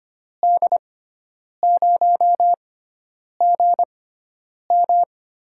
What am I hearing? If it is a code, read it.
Morse code, heard as a keyed tone, about 25 wpm, spaced as for 9 wpm, 700 Hz. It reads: D0GM
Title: USA Call Signs - 6 Characters - 6x Character Spacing - 25wpm